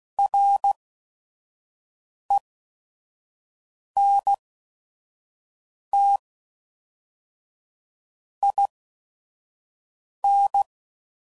Morse code, heard as a keyed tone, about 16 wpm, spaced as for 5 wpm, 800 Hz. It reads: RENT IN
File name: CW Code Groups @5WPM / Technical Subjects / XC